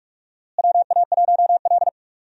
Morse code, heard as a keyed tone, about 46 wpm, 700 Hz. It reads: WA1L
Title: USA Call Signs - 4 Characters - Repeated 3x - 45wpm